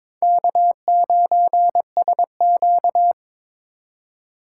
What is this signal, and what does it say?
Morse code, heard as a keyed tone, about 22 wpm, 700 Hz. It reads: K9SQ
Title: USA Call Signs - 4 Characters - 3x Character Spacing - 22wpm